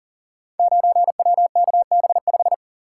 Morse code, heard as a keyed tone, about 40 wpm, 700 Hz. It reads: 9WKB5
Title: Code Groups - 5 Letter:Numbers - 40wpm